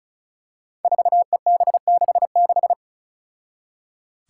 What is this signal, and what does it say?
Morse code, heard as a keyed tone, about 36 wpm, 700 Hz. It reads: 4EB66
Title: Five Challenging Letter-Number Combinations 35wpm